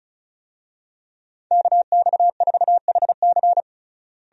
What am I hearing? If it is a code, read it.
Morse code, heard as a keyed tone, about 34 wpm, 700 Hz. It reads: KX4HC
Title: USA Call Signs - 5 Characters - Repeated 2x - 35wpm